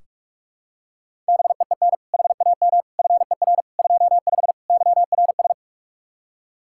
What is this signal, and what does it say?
Morse code, heard as a keyed tone, about 46 wpm, 700 Hz. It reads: BEEN HAM FER 25 YRS
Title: Practice QSO - Exchange 3 - More Info - 45wpm